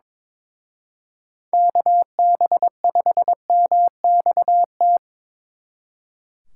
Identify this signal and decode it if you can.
Morse code, heard as a keyed tone, about 26 wpm, 700 Hz. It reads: KB5MXT
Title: Speed Racing - USA Call Signs 22wpm